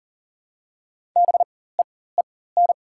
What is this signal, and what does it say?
Morse code, heard as a keyed tone, about 39 wpm, spaced as for 19 wpm, 700 Hz. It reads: BEEN
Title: 4x Character Spacing - Top 300 Words - 40wpm